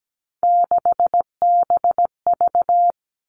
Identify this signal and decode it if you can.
Morse code, heard as a keyed tone, about 17 wpm, 700 Hz. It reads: 6BV
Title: Three Challenging Letter-Number Combinations 17wpm